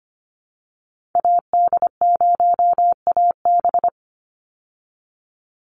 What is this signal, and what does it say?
Morse code, heard as a keyed tone, about 25 wpm, 700 Hz. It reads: AD0AB
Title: Call Signs Repeated 3x - 25wpm